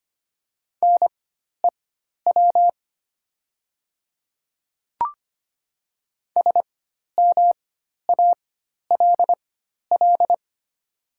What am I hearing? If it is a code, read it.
Morse code, heard as a keyed tone, about 25 wpm, spaced as for 12 wpm, 690 Hz. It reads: NEW SMALL
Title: 4x Character Spacing - Top 300 Words - 25wpm